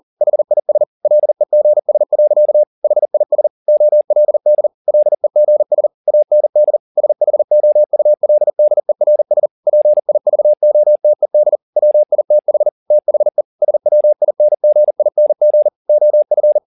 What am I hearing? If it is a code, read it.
Morse code, heard as a keyed tone, about 40 wpm, 600 Hz. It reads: HIS LEGS. HIS OLD LEGS AND SHOULDERS PIVOTED WITH THE SWINGING OF
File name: the old man and the sea 2 40wpm